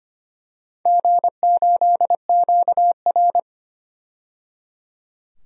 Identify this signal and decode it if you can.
Morse code, heard as a keyed tone, about 25 wpm, 700 Hz. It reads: G8QR
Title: DXCC-100 Call Signs - 4 Characters - 25wpm